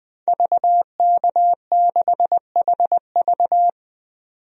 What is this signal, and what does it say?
Morse code, heard as a keyed tone, about 20 wpm, 700 Hz. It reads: VK6HV
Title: DXCC-100 Call Signs - 5 Characters - 20wpm